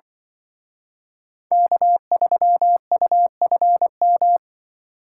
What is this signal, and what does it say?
Morse code, heard as a keyed tone, about 24 wpm, 700 Hz. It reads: K3UFM
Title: Speed Racing - USA Call Signs 20wpm